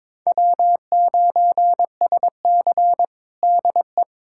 Morse code, heard as a keyed tone, about 22 wpm, 700 Hz. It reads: W9SC DE
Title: USA Callsign DE Callsign - 4 Characters - 22wpm